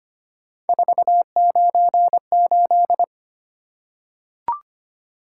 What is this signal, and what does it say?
Morse code, heard as a keyed tone, about 25 wpm, 700 Hz. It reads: 498
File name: Three Letter-Number Combinations - 25wpm